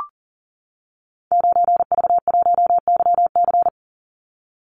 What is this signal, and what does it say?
Morse code, heard as a keyed tone, about 40 wpm, 700 Hz. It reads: 9V1YC
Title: Call Signs - 40wpm